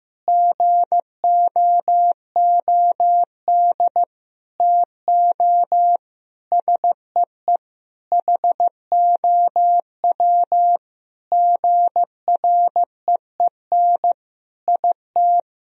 Morse code, heard as a keyed tone, about 15 wpm, 700 Hz. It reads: GOOD TO SEE HOW GREEN IT
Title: Sentences from Top 1000 Words - Review Part 2 - 15wpm